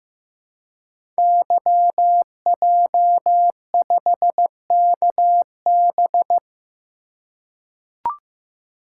Morse code, heard as a keyed tone, about 15 wpm, 700 Hz. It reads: YJ5KB
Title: Code Groups - 5 Letter:Numbers - 15wpm